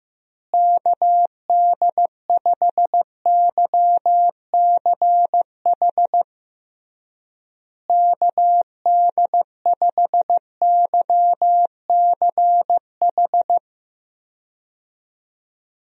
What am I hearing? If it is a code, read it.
Morse code, heard as a keyed tone, about 15 wpm, 700 Hz. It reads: KD5YCH KD5YCH
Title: USA Call Signs - 6 Characters - Repeated 3x - 15wpm